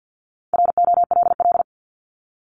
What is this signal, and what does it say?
Morse code, heard as a keyed tone, about 50 wpm, 700 Hz. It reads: ROLL
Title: Multi-letter Phonemes Repeated 3x 50wpm